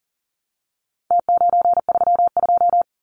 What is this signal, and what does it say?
Morse code, heard as a keyed tone, about 38 wpm, 700 Hz. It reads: T932
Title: Numbers - Contest Style - 4-Digit 40wpm